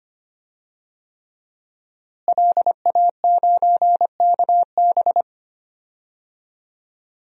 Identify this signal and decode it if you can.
Morse code, heard as a keyed tone, about 25 wpm, 700 Hz. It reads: LA9KB
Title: DXCC-100 Call Signs - 5 Characters - Repeated 3x - 25wpm